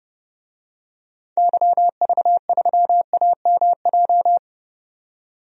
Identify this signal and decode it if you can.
Morse code, heard as a keyed tone, about 30 wpm, 700 Hz. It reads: YV3AMJ
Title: DXCC-100 Call Signs - 6 Characters - 30wpm